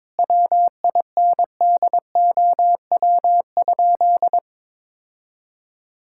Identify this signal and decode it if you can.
Morse code, heard as a keyed tone, about 22 wpm, 700 Hz. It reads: WINDOW?
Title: Sentences from Top 1600 Words 22wpm